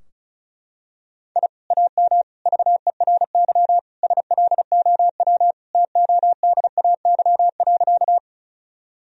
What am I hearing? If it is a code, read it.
Morse code, heard as a keyed tone, about 36 wpm, 700 Hz. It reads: I AM VERY SLOW TODAY.
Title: Sentences from Top 1000 Words - Review Part 2 - 35wpm